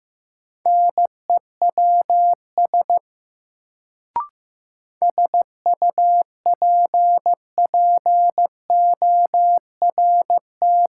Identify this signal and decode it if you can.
Morse code, heard as a keyed tone, about 15 wpm, 700 Hz. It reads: NEWS SUPPORT
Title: Sentences from Top 800 Words - New Word Drill - 15wpm